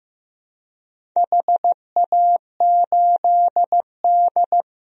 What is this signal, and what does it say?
Morse code, heard as a keyed tone, about 15 wpm, 700 Hz. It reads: HA8D
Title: DXCC-100 Call Signs - 4 Characters - Repeated 3x - 15wpm